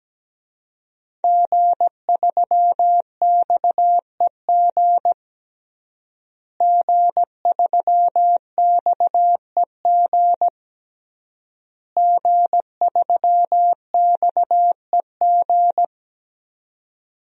Morse code, heard as a keyed tone, about 17 wpm, 700 Hz. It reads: G3XEG G3XEG G3XEG
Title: DXCC-100 Call Signs - 5 Characters - Repeated 3x - 17wpm